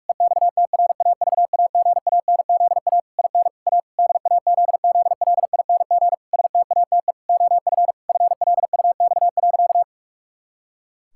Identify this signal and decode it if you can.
Morse code, heard as a keyed tone, about 46 wpm, 700 Hz. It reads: EXTRAVAGANZA IN A DAZZLING STATE OF FLUX.
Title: Pangram Challenge v2 - 45wpm